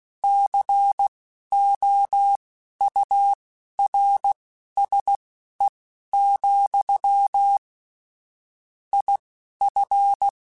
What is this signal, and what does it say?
Morse code, heard as a keyed tone, about 16 wpm, spaced as for 11 wpm, 800 Hz. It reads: COURSE, IF
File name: QSOs / QSOs-mp3s / WA7QOX 1@10